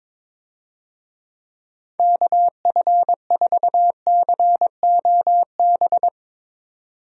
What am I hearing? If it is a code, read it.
Morse code, heard as a keyed tone, about 22 wpm, 700 Hz. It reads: KF4COB
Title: USA Call Signs - 6 Characters - 22wpm